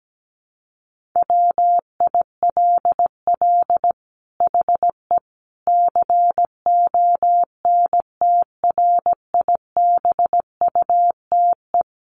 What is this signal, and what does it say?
Morse code, heard as a keyed tone, about 17 wpm, 700 Hz. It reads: WILL HE CONTRIBUTE
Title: Sentences from Top 1400 Words 17wpm